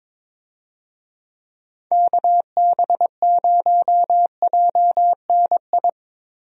Morse code, heard as a keyed tone, about 22 wpm, 700 Hz. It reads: KB0JNI
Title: USA Call Signs - 6 Characters - Repeated 2x - 22wpm